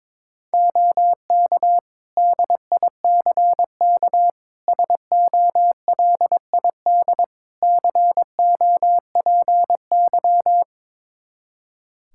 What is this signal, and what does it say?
Morse code, heard as a keyed tone, about 22 wpm, 700 Hz. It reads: OK DICK SOLID COPY
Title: Practice QSO - Exchange 3 - More Info - 22wpm